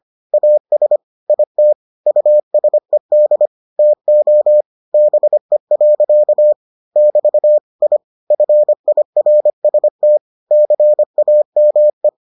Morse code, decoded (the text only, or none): AS IT USED TO BE. <BT> I FIRST CAME